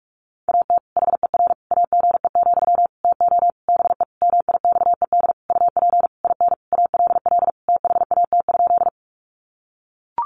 Morse code, heard as a keyed tone, about 44 wpm, 700 Hz. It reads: AT HER AGE, TO BE MIXED UP IN ALL THAT?
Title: Sentences from Top 1500 Words 45wpm